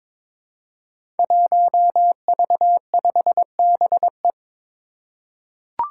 Morse code, heard as a keyed tone, about 22 wpm, 700 Hz. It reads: 1V5BE
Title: Five Challenging Letter-Number Combinations 22wpm